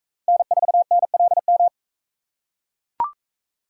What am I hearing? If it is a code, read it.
Morse code, heard as a keyed tone, about 41 wpm, 700 Hz. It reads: N4NLM